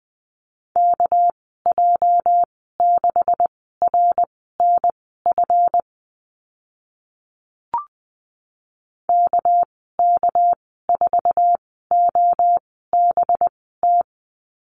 Morse code, heard as a keyed tone, about 20 wpm, spaced as for 14 wpm, 700 Hz. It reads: KJ6RNF KK4OBT
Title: 213 Lesson 34g - Number 8 - Call Signs 20wpm